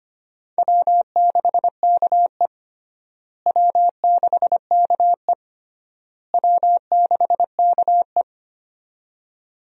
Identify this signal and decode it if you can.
Morse code, heard as a keyed tone, about 25 wpm, 700 Hz. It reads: W6KE W6KE W6KE
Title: USA Call Signs - 4 Characters - Repeated 3x - 25wpm